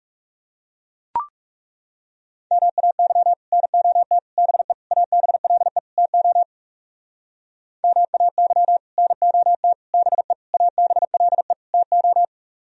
Morse code, heard as a keyed tone, about 36 wpm, 700 Hz. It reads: MAY NOT BE ABLE TO MAY NOT BE ABLE TO
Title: Speed-Racing - Sets of 5 Words 30wpm